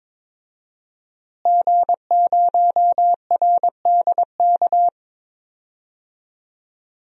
Morse code, heard as a keyed tone, about 21 wpm, 700 Hz. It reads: G0RDK
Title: DXCC-100 Call Signs - 5 Characters - Repeated 2x - 22wpm